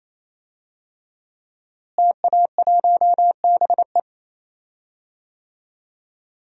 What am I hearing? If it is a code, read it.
Morse code, heard as a keyed tone, about 28 wpm, 700 Hz. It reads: TA1BE
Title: DXCC-100 Call Signs - 5 Characters - Repeated 3x - 28wpm